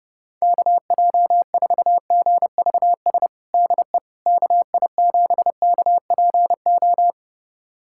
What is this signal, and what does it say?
Morse code, heard as a keyed tone, about 30 wpm, 700 Hz. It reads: KJ4GVS DE KI7KPO
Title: USA Callsign DE Callsign - 6 Characters - 30wpm